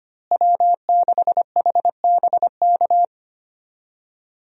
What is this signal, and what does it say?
Morse code, heard as a keyed tone, about 25 wpm, 700 Hz. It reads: W6HBK
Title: USA Call Signs - 5 Characters - Repeated 3x - 25wpm